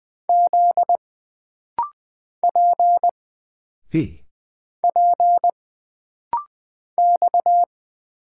Morse code, heard as a keyed tone, about 20 wpm, 700 Hz. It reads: Z P P X
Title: Single Character (Letter:Number:Punctuation:Prosign) - Rapid-Fire - 20wpm